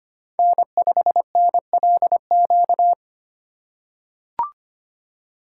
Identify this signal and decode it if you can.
Morse code, heard as a keyed tone, about 25 wpm, 700 Hz. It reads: N5NLQ
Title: USA Call Signs - 5 Characters - 2x Character Spacing - 25wpm